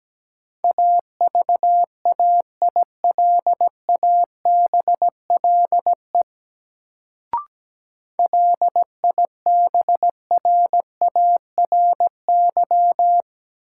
Morse code, heard as a keyed tone, about 17 wpm, 700 Hz. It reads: AVAILABLE LIBRARY
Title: Sentences from Top 600 Words - New Word Drill - 17wpm